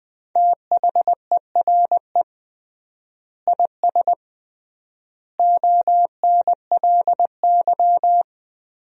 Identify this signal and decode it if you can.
Morse code, heard as a keyed tone, about 20 wpm, 700 Hz. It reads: THERE IS ONLY
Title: 3x Word-Spacing - Sentences from Top 300 Words - 20wpm